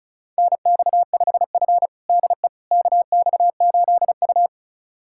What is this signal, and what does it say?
Morse code, heard as a keyed tone, about 34 wpm, 700 Hz. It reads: NX5F DE KX8U
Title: USA Callsign DE Callsign - 4 Characters - 35wpm